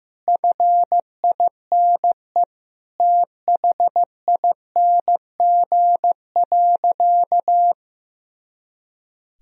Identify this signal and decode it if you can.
Morse code, heard as a keyed tone, about 16 wpm, 700 Hz. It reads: FINE THING.
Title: Sentences from Top 500 Words and Abbreviations 15wpm